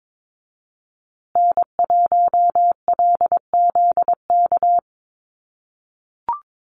Morse code, heard as a keyed tone, about 22 wpm, 700 Hz. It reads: N1LZK
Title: USA Call Signs Repeated 2x 22wpm